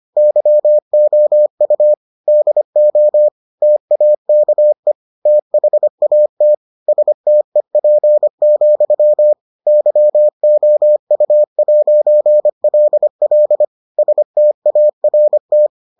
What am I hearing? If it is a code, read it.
Morse code, heard as a keyed tone, about 25 wpm, 600 Hz. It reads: YOU DO TAKE THAT STEP, YOU'LL START